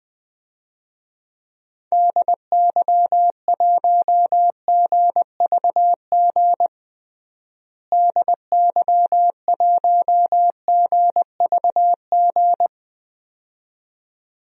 Morse code, heard as a keyed tone, about 21 wpm, 700 Hz. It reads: DY1GVG DY1GVG
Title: DXCC-100 Call Signs - 6 Characters - Repeated 2x - 20wpm